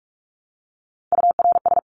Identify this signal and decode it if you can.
Morse code, heard as a keyed tone, about 44 wpm, 700 Hz. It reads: URS
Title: Abbreviations 45wpm